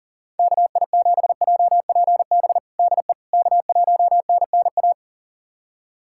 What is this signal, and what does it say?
Morse code, heard as a keyed tone, about 40 wpm, 700 Hz. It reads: KI7JPB DE K1NNA
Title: USA Callsign DE Callsign 40wpm